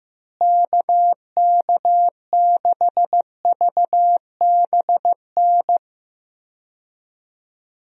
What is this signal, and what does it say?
Morse code, heard as a keyed tone, about 15 wpm, 700 Hz. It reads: KK6VBN
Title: USA Call Signs - 6 Characters - Repeated 3x - 15wpm